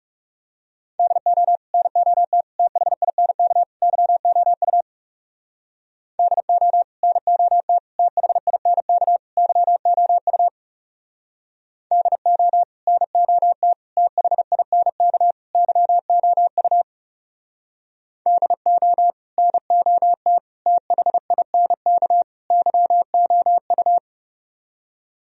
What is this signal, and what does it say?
Morse code, heard as a keyed tone, about 36 wpm, 700 Hz. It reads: DO NOT THINK YOU DO NOT THINK YOU DO NOT THINK YOU DO NOT THINK YOU
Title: Speed-Racing - Sets of 4 Words 30wpm